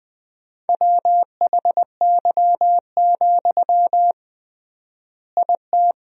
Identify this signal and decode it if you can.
Morse code, heard as a keyed tone, about 21 wpm, 700 Hz. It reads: WHY, IT
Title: 3x Word-Spacing - Sentences from Top 300 Words - 20wpm